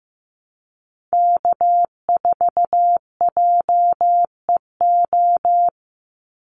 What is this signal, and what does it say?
Morse code, heard as a keyed tone, about 15 wpm, 700 Hz. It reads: K4JEO
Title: USA Call Signs 15wpm